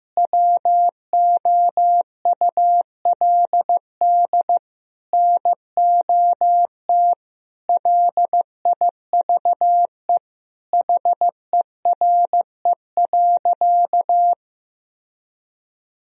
Morse code, heard as a keyed tone, about 15 wpm, 700 Hz. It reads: WOULD NOT LIVE HERE.
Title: Sentences from Top 1000 Words - Review Part 1 - 15wpm